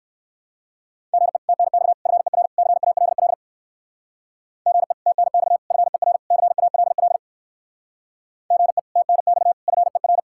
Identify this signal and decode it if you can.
Morse code, heard as a keyed tone, about 58 wpm, 700 Hz. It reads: GE TNX FER CALL GE TNX FER CALL GE TNX FER